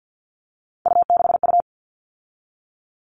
Morse code, heard as a keyed tone, about 50 wpm, 700 Hz. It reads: U6U
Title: All Cut Numbers - 3-Digit 50wpm